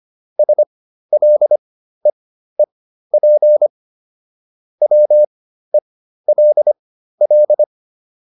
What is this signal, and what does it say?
Morse code, heard as a keyed tone, about 25 wpm, spaced as for 13 wpm, 600 Hz. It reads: SLEEP WELL
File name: PR404 13